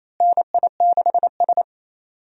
Morse code, heard as a keyed tone, about 28 wpm, 700 Hz. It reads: NI6S
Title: USA Call Signs - 4 Characters - Repeated 3x - 28wpm